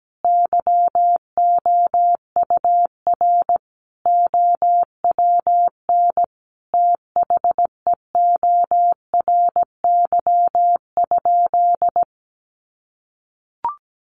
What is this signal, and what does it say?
Morse code, heard as a keyed tone, about 17 wpm, 700 Hz. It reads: YOUR OWN THEORY?
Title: Sentences from Top 1000 Words 17wpm